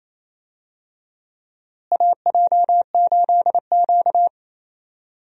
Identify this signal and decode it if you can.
Morse code, heard as a keyed tone, about 28 wpm, 700 Hz. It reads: AJ8Q
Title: USA Call Signs - 4 Characters - Repeated 3x - 28wpm